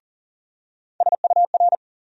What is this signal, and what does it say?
Morse code, heard as a keyed tone, about 40 wpm, 700 Hz. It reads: SUR
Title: USA Call Sign Postfixes - 3 Letters 40wpm